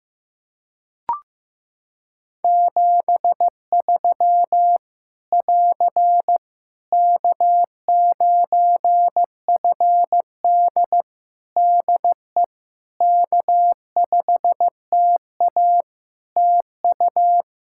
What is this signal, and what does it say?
Morse code, heard as a keyed tone, about 15 wpm, 700 Hz. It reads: 73 <AR> K9FD DE K5TA TU
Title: K1USN SST QSO Practice 15wpm